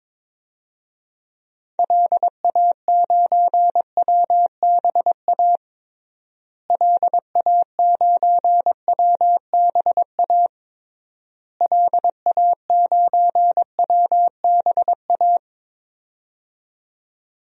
Morse code, heard as a keyed tone, about 22 wpm, 700 Hz. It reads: LA9WBA LA9WBA LA9WBA
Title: DXCC-100 Call Signs - 6 Characters - Repeated 3x - 22wpm